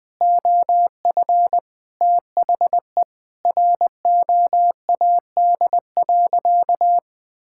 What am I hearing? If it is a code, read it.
Morse code, heard as a keyed tone, about 20 wpm, 700 Hz. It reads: OF THE ROAD.
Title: Sentences with 3-Word Phrases - Spoken Beginning - 20wpm